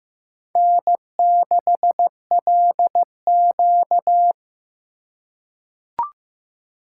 Morse code, heard as a keyed tone, about 15 wpm, 700 Hz. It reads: N6LQ
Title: USA Call Signs - 4 Characters - 3x Character Spacing - 15wpm